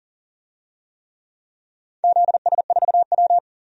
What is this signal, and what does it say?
Morse code, heard as a keyed tone, about 37 wpm, 700 Hz. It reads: ZS4W
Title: DXCC-100 Call Signs - 4 Characters - Repeated 3x - 40wpm